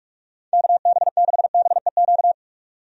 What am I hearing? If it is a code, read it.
Morse code, heard as a keyed tone, about 46 wpm, 700 Hz. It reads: KB6BEQ
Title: USA Call Signs - 6 Characters - 45wpm